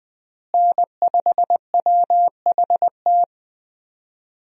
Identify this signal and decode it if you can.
Morse code, heard as a keyed tone, about 20 wpm, 700 Hz. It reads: N5WHT
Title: USA Call Signs - 5 Characters - Repeated 2x - 20wpm